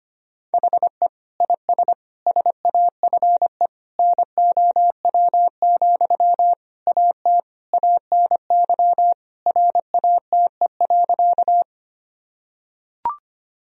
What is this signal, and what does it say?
Morse code, heard as a keyed tone, about 25 wpm, 700 Hz. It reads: HE IS SAFE NOW, AT ANY RATE.
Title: Sentences from Top 1000 Words - Review Part 4 - 25wpm